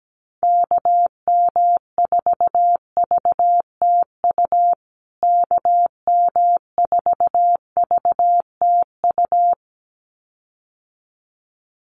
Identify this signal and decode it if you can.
Morse code, heard as a keyed tone, about 17 wpm, 700 Hz. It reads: KM4VTU KM4VTU
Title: USA Call Signs Repeated 2x 17wpm